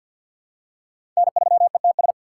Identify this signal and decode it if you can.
Morse code, heard as a keyed tone, about 45 wpm, 700 Hz. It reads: N3ETS
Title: USA Call Signs - 5 Characters - 4x Character Spacing - 50wpm